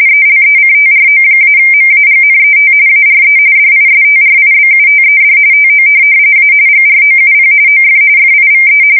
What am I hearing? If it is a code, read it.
RTTY, about 45.45 baud. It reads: ELECTRO-MAGNET ARE DISTRIBUTED TO A GROUP OF FIVE
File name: fsk170 / morkrum